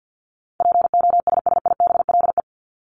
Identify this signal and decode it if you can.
Morse code, heard as a keyed tone, about 50 wpm, 700 Hz. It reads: POSSIBLE
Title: LY Suffix - Top 100 Words with Base Words 50wpm